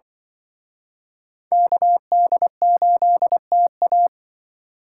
Morse code, heard as a keyed tone, about 24 wpm, 700 Hz. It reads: KD8TA
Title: Speed Racing - USA Call Signs 20wpm